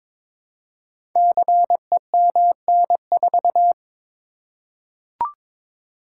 Morse code, heard as a keyed tone, about 22 wpm, 700 Hz. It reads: CEMN4
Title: Code Groups - 5 Letter:Numbers - 22wpm